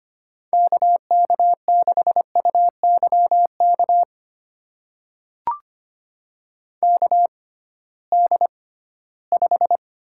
Morse code, heard as a keyed tone, about 25 wpm, spaced as for 16 wpm, 700 Hz. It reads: KK6UYK KD5
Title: USA Call Signs - 6 Characters - 6x Character Spacing - 25wpm